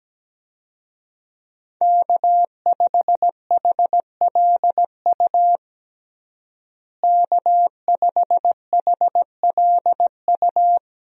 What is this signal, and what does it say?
Morse code, heard as a keyed tone, about 17 wpm, 700 Hz. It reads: K5HLU K5HLU
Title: USA Call Signs - 5 Characters - Repeated 2x - 17wpm